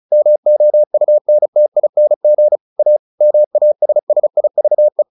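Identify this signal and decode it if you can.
Morse code, heard as a keyed tone, about 34 wpm, 600 Hz. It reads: MOUNTING A MASSIVE
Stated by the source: Morse practice at 35 wpm from CW med Gnister, sending Ken Follett's novel Never in English.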